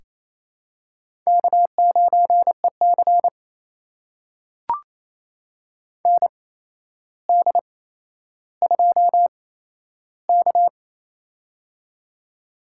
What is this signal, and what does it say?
Morse code, heard as a keyed tone, about 28 wpm, spaced as for 12 wpm, 700 Hz. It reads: K9EC ND2K
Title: USA Call Signs - 8x Character Spacing - 28wpm